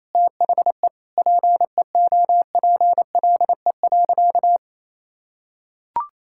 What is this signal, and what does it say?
Morse code, heard as a keyed tone, about 28 wpm, 700 Hz. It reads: THE PEOPLE.
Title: Sentences from Top 1000 Words - Review Part 6 - 28wpm